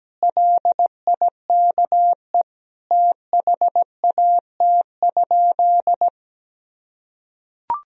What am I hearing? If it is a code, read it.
Morse code, heard as a keyed tone, about 17 wpm, 700 Hz. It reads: LIKE THAT?
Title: Sentences from Top 1000 Words - Review Part 1 - 17wpm